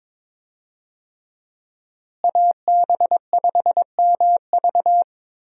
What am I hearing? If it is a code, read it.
Morse code, heard as a keyed tone, about 22 wpm, 700 Hz. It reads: AB5MV
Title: USA Call Signs - 5 Characters - Repeated 3x - 22wpm